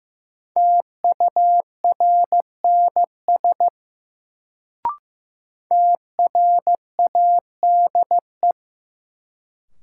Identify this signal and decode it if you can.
Morse code, heard as a keyed tone, about 16 wpm, 700 Hz. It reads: TURNS TRADE
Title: Sets of 5 Words Encore - New Word Drill - 15wpm